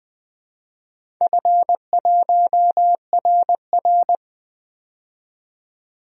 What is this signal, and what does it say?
Morse code, heard as a keyed tone, about 20 wpm, 700 Hz. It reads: F1RR
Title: DXCC-100 Call Signs - 4 Characters - 20wpm